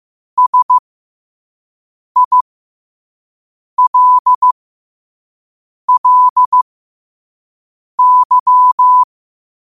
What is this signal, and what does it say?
Morse code, heard as a keyed tone, about 15 wpm, spaced as for 5 wpm, 1000 Hz. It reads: SILLY
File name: qotdfiles / 5wpm